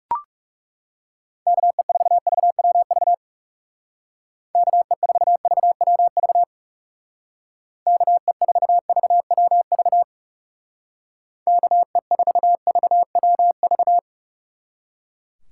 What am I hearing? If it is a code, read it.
Morse code, heard as a keyed tone, about 39 wpm, 700 Hz. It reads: KE4VWV KE4VWV KE4VWV KE4VWV